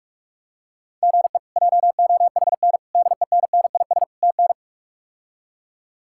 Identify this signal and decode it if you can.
Morse code, heard as a keyed tone, about 42 wpm, 700 Hz. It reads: GE JOHN DENNIS TN
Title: K1USN SST QSO Practice - 45wpm